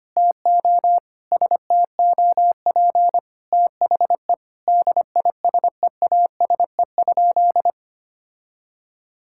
Morse code, heard as a keyed tone, about 25 wpm, 700 Hz. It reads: TO STOP THE DISEASE?
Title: Sentences from Top 1000 Words - Review Part 6 - 25wpm